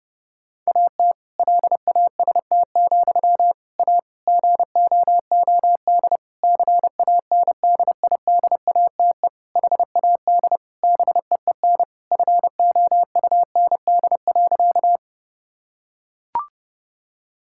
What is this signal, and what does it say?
Morse code, heard as a keyed tone, about 32 wpm, 700 Hz. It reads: AT LAST, A GOOD CANDIDATE HAD BEEN FOUND.
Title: Sentences from Top 1600 Words 30wpm